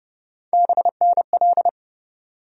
Morse code, heard as a keyed tone, about 30 wpm, 700 Hz. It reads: BNL
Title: USA Call Sign Postfixes - 3 Letters 30wpm